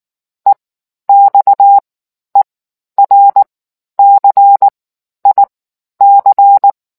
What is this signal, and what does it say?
Morse code, heard as a keyed tone, about 19 wpm, spaced as for 11 wpm, 800 Hz. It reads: EXERCIC